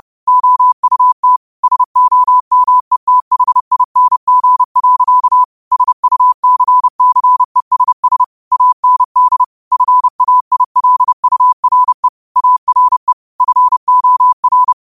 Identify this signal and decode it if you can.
Morse code, heard as a keyed tone, about 30 wpm, 1000 Hz. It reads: OUT SOMETHING. SUCCESS AND FAILURE ARE FOR
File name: qotdfiles / 30wpm